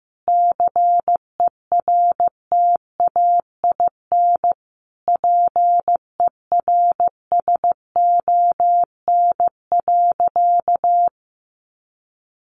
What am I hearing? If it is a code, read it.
Morse code, heard as a keyed tone, about 14 wpm, 700 Hz. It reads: CERTAIN PERSON.
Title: Sentences from Top 500 Words 15wpm